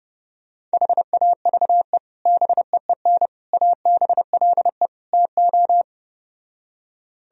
Morse code, heard as a keyed tone, about 31 wpm, 700 Hz. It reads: HAVE BEEN ABLE TO